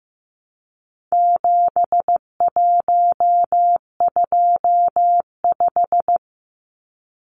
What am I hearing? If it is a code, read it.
Morse code, heard as a keyed tone, about 15 wpm, 700 Hz. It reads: 7125
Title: Numbers - 4-Digit 15wpm